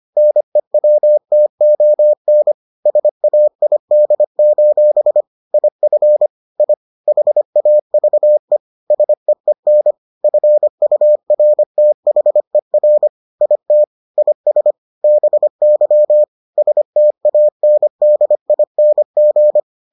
Morse code, heard as a keyed tone, about 25 wpm, 600 Hz. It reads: NEWTON SAID: IF I HAVE SEEN FURTHER IT IS BY STANDING